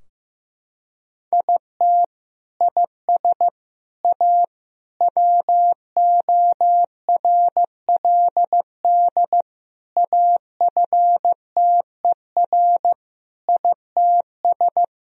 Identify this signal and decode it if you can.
Morse code, heard as a keyed tone, about 15 wpm, 700 Hz. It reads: IT IS A WORLD AFTER ITS
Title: Sentences from Top 1000 Words - Review Part 4 - 15wpm